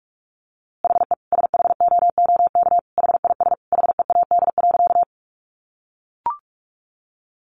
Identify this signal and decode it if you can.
Morse code, heard as a keyed tone, about 44 wpm, 700 Hz. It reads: HE SHOOK HIS HEAD.
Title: A Sound Of Thunder 45wpm